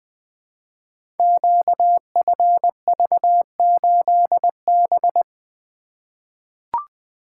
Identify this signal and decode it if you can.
Morse code, heard as a keyed tone, about 20 wpm, 700 Hz. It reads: QFV8B
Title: Code Groups - 5 Letter:Numbers - 20wpm